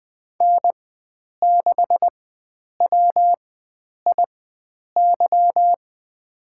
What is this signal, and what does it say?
Morse code, heard as a keyed tone, about 20 wpm, spaced as for 9 wpm, 700 Hz. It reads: N6WIY
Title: USA Call Signs - 4x Character Spacing - 20wpm